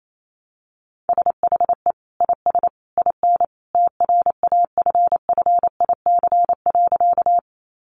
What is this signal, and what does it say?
Morse code, heard as a keyed tone, about 28 wpm, 700 Hz. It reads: SHE IS IN TRAFFIC.
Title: Sentences from Top 1300 Words 28wpm